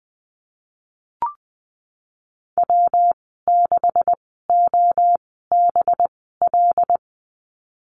Morse code, heard as a keyed tone, about 20 wpm, spaced as for 15 wpm, 700 Hz. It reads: W6OBL
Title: 186 Lesson 30g - Letter b - Call Signs 20wpm